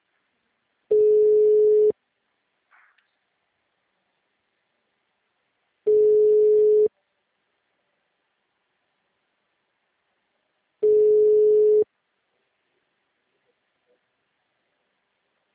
Ringback tone (Europe, 425 Hz).
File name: out- -2016- - - .77